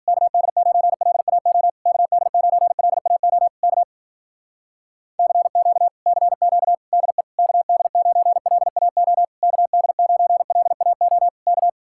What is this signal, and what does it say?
Morse code, heard as a keyed tone, about 52 wpm, 700 Hz. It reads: KD9LAO KD9LAO K CQ CQ DE KD9LAO KD9LAO K